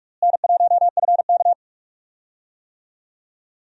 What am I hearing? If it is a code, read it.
Morse code, heard as a keyed tone, about 46 wpm, 700 Hz. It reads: N1FK
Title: USA Call Signs - 4 Characters - Repeated 3x - 45wpm